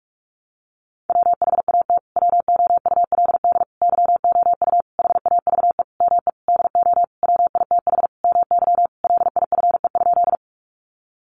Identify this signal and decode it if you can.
Morse code, heard as a keyed tone, about 45 wpm, 700 Hz. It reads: WHAT WOULD YOU HAVE ME DO WITH MY LIFE?
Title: Sentences from Top 300 Words 45wpm